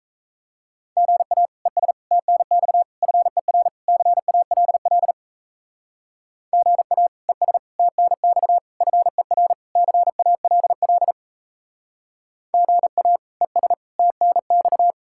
Morse code, heard as a keyed tone, about 35 wpm, 700 Hz. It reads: GA ES TNX FER CALL GA ES TNX FER CALL GA ES TNX
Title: Speed-Racing Common QSO Phrases - 28wpm